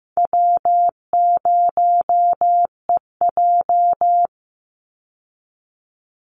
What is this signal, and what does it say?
Morse code, heard as a keyed tone, about 15 wpm, 700 Hz. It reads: W0EJ
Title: Easy Call Signs Repeated 3x - 15wpm